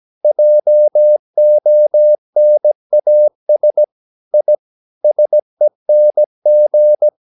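Morse code, heard as a keyed tone, about 17 wpm, 600 Hz. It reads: JONAS I SENG
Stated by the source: Slow (QRS) Morse practice at 17 wpm, from CW med Gnister.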